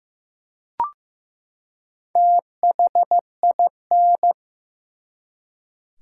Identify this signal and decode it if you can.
Morse code, heard as a keyed tone, about 15 wpm, 700 Hz. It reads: THIN
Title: Four Letter Words - 15wpm